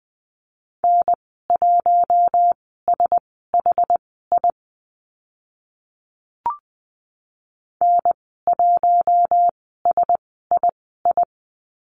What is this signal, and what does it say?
Morse code, heard as a keyed tone, about 20 wpm, spaced as for 14 wpm, 700 Hz. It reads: N1SHI N1SII
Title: 062 Lesson 11g - Letter h - Call Signs 20wpm